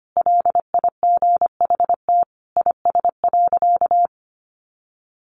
Morse code, heard as a keyed tone, about 25 wpm, 700 Hz. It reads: LIGHT IS.
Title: Sentences from Top 1000 Words 25wpm